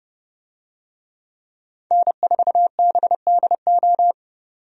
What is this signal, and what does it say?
Morse code, heard as a keyed tone, about 29 wpm, 700 Hz. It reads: N4BDO
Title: USA Call Signs - 5 Characters - 30wpm